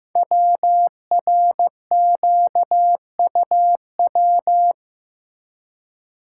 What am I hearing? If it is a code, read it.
Morse code, heard as a keyed tone, about 15 wpm, 700 Hz. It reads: WRQUW
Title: Code Groups - 5 Letters - 15wpm